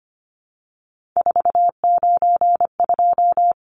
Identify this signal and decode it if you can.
Morse code, heard as a keyed tone, about 25 wpm, 700 Hz. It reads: 492
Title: Numbers - 3-Digit 25wpm